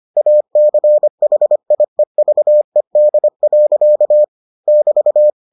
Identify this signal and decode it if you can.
Morse code, heard as a keyed tone, about 25 wpm, 600 Hz. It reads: ACHIEVED. <BT>